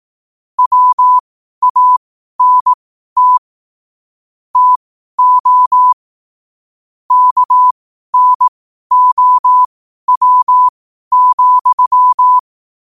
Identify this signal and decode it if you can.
Morse code, heard as a keyed tone, about 18 wpm, spaced as for 12 wpm, 1000 Hz. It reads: WANT TO KNOW,
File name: qotdfiles / 12wpm